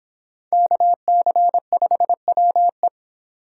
Morse code, heard as a keyed tone, about 23 wpm, 700 Hz. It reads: KC5WE
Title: Speed Racing - USA 2x2 Call Signs 22wpm